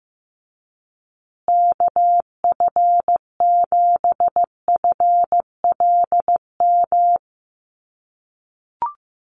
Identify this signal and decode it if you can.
Morse code, heard as a keyed tone, about 15 wpm, 700 Hz. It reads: KF7FLM
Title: USA Call Signs Repeated 3x 15wpm